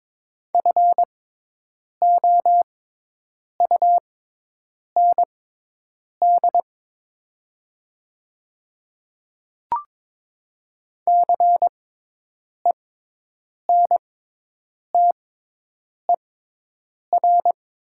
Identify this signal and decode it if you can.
Morse code, heard as a keyed tone, about 22 wpm, spaced as for 8 wpm, 700 Hz. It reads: FOUND CENTER
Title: 6x Character Spacing - Top 500 Words - 22wpm